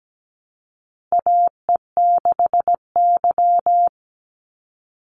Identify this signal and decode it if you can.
Morse code, heard as a keyed tone, about 17 wpm, 700 Hz. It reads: AE6Y
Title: Call Signs Repeated 2x - 17wpm